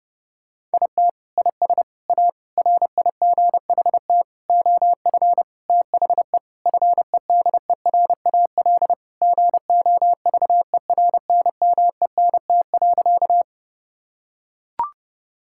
Morse code, heard as a keyed tone, about 30 wpm, 700 Hz. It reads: IT IS A RIGHT OF THE FEDERAL GOVERNMENT.
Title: Sentences from Top 1000 Words - Review Part 5 - 30wpm